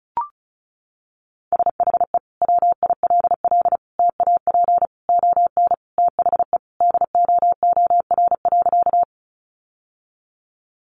Morse code, heard as a keyed tone, about 34 wpm, 700 Hz. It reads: SHE WILL TAP ON THE DOOR.
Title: Sentences from Top 1400 Words 35wpm